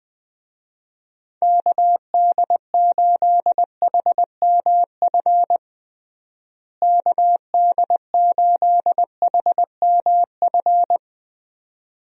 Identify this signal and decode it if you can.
Morse code, heard as a keyed tone, about 20 wpm, 700 Hz. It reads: KD8HMF KD8HMF
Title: USA Call Signs - 6 Characters - Repeated 2x - 20wpm